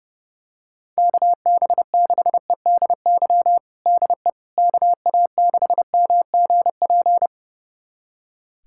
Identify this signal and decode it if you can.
Morse code, heard as a keyed tone, about 32 wpm, 700 Hz. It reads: KB6EDY DE KA6MGP
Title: USA Callsign DE Callsign - 6 Characters - 30wpm